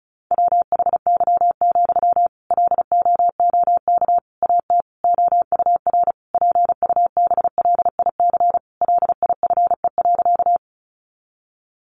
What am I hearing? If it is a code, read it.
Morse code, heard as a keyed tone, about 34 wpm, 700 Hz. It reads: WHY, LOOK AT OUR PUBLIC LIFE.
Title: Sentences from Top 600 Words 35wpm